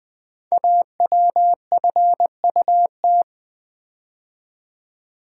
Morse code, heard as a keyed tone, about 20 wpm, 700 Hz. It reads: AWFUT
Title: Code Groups - 5 Letters - 20wpm